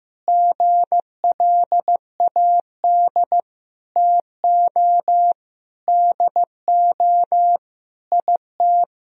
Morse code, heard as a keyed tone, about 15 wpm, 700 Hz. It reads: GLAD TO DO IT
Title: Sentences from Top 1000 Words - Review Part 3 - 15wpm